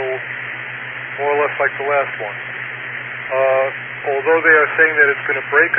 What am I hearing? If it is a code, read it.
Emergency Alert System (SAME) tones (header bursts), heard as harsh buzzing bursts, with speech. It is heard from 0.3 to 5.5 s.